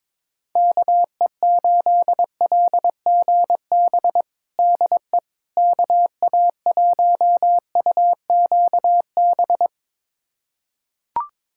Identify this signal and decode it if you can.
Morse code, heard as a keyed tone, about 22 wpm, 700 Hz. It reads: KE8LGB DE KA1UQB
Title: USA Callsign DE Callsign - 6 Characters - 22wpm